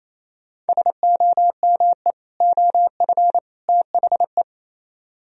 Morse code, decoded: SOME OF THE

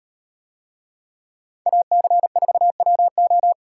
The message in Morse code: AC4WO

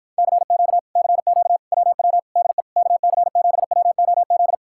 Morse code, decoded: CQ CQ WW DE KC7WOZ

